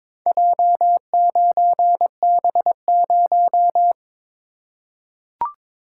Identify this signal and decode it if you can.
Morse code, heard as a keyed tone, about 23 wpm, 700 Hz. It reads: J9B0
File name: practice / 22wpm / Four Letter-Number Combinations - 22wpm